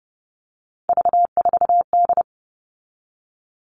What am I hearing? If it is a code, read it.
Morse code, heard as a keyed tone, about 30 wpm, 700 Hz. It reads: V4D
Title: All Cut Numbers - 3-Digit 30wpm